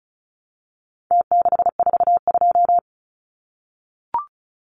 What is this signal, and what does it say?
Morse code, heard as a keyed tone, about 34 wpm, 700 Hz. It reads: T642
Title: Numbers - Contest Style - 4-Digit 35wpm